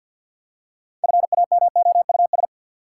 Morse code, heard as a keyed tone, about 53 wpm, 700 Hz. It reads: FAMOUS
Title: Six Letter Words - 50wpm